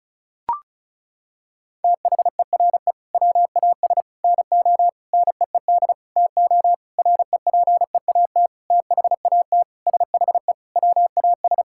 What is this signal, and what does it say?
Morse code, heard as a keyed tone, about 34 wpm, 700 Hz. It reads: THERE WAS NO NEED TO REPEAT THAT SHE WAS
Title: Sentences from Top 1600 Words 35wpm